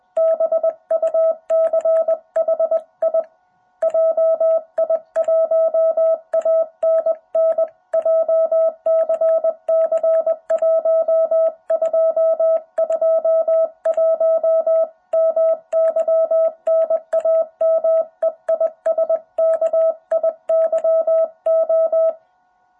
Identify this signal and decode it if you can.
Morse code, heard as a keyed tone, about 20 wpm, 660 Hz. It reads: BUCHI JI1ANN JCC 1221 MY NAME IS KIYO
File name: jh8cbh / 2021cw